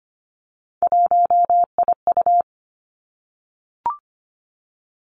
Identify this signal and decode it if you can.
Morse code, heard as a keyed tone, about 25 wpm, 700 Hz. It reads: 1IU
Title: Three Challenging Letter-Number Combinations 25wpm